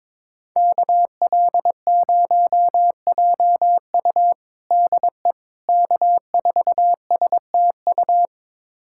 Morse code, heard as a keyed tone, about 22 wpm, 700 Hz. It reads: KL0JU DE K4STU
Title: USA Callsign DE Callsign - 5 Characters - 22wpm